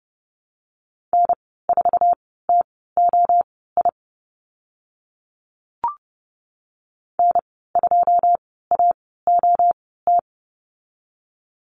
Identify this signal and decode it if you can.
Morse code, heard as a keyed tone, about 30 wpm, spaced as for 17 wpm, 700 Hz. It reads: N4TOI N2AOT
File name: Lesson 14g - Number 2 - Call Signs 30wpm